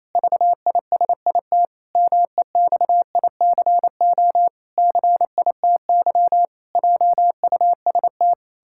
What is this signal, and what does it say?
Morse code, heard as a keyed tone, about 27 wpm, 700 Hz. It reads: VISIT MEXICO CITY JUST
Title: Pangram Challenge v2 - 28wpm